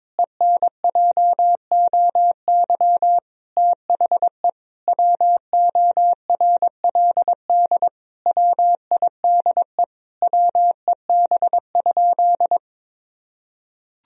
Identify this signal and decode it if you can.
Morse code, heard as a keyed tone, about 22 wpm, 700 Hz. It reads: ENJOY THE WORLD WIDE WEB?
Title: Sentences from Top 1000 Words - Review Part 4 - 22wpm